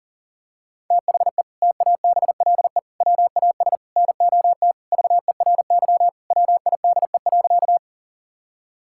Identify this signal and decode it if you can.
Morse code, heard as a keyed tone, about 40 wpm, 700 Hz. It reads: THE TABLE WAS NOT VERY WIDE.
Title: Sentences from Top 1000 Words - Review Part 6 - 40wpm